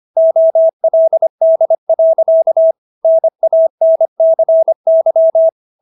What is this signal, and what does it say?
Morse code, heard as a keyed tone, about 25 wpm, 650 Hz. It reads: OLD. NANCY